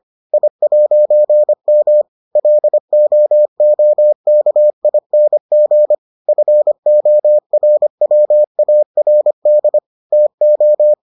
Morse code, heard as a keyed tone, about 25 wpm, 600 Hz. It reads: I'M LOOKING FORWARD TO